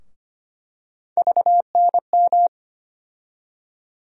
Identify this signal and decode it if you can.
Morse code, heard as a keyed tone, about 25 wpm, 700 Hz. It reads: VNM
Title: USA Call Sign Postfixes - 3 Letters 25wpm